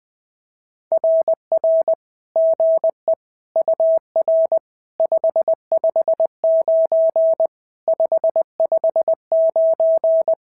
Morse code, heard as a keyed tone, about 20 wpm, 650 Hz. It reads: RR GE UR 559 559
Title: SOTA QSO Practice 20wpm